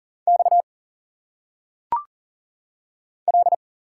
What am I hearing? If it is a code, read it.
Morse code, heard as a keyed tone, about 40 wpm, 700 Hz. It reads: X L
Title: Single Letters - Phonetic Pronunciation 40wpm